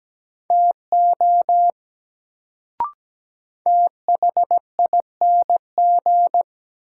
Morse code, heard as a keyed tone, about 17 wpm, 700 Hz. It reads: TO THING
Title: Sentences from Top 100 Words - New Word Drill - 17wpm